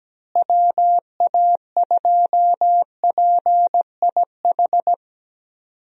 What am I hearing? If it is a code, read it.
Morse code, heard as a keyed tone, about 17 wpm, 700 Hz. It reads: WA2PIH
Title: USA Call Signs - 6 Characters - Repeated 3x - 17wpm